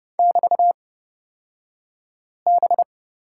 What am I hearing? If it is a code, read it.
Morse code, heard as a keyed tone, about 30 wpm, spaced as for 5 wpm, 700 Hz. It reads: <BT>B
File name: Lesson 37d - Letter x - Two Characters 30wpm